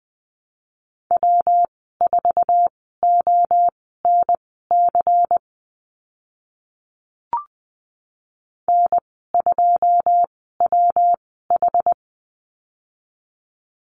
Morse code, heard as a keyed tone, about 20 wpm, spaced as for 14 wpm, 700 Hz. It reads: W4ONC N2WH
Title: 140 Lesson 23g - Letter f - Call Signs 20wpm